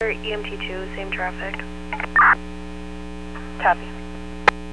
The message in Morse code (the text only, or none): none